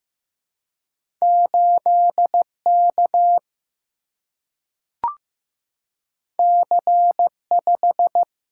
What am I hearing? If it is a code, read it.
Morse code, heard as a keyed tone, about 15 wpm, 700 Hz. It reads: 8 K C 5
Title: Single-Letter Single-Number 15wpm